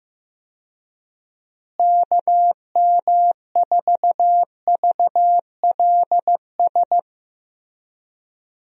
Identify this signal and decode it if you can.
Morse code, heard as a keyed tone, about 15 wpm, 700 Hz. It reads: KM4VLS
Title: USA Call Signs - 6 Characters - Repeated 3x - 15wpm